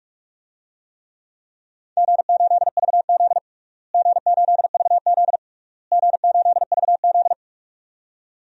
Morse code, heard as a keyed tone, about 42 wpm, 700 Hz. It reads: G8VZ G8VZ G8VZ
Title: DXCC-100 Call Signs - 4 Characters - Repeated 3x - 45wpm